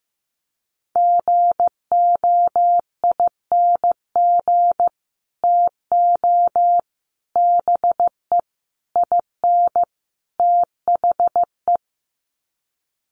Morse code, heard as a keyed tone, about 15 wpm, 700 Hz. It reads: GOING TO BE IN THE
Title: Sets of 5 Words Encore 15wpm